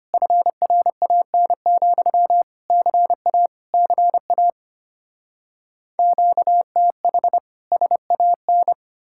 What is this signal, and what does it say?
Morse code, heard as a keyed tone, about 33 wpm, 700 Hz. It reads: FRAN, CA CA QTH SAN